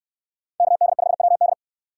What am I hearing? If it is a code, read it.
Morse code, heard as a keyed tone, about 66 wpm, 700 Hz. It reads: KD5WD